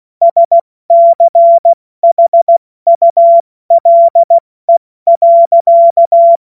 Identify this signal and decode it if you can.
Morse code, heard as a keyed tone, about 16 wpm, 680 Hz. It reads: SCHULE.
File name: Morsekurs / Witze / Witz 10. August 2021, 70 F80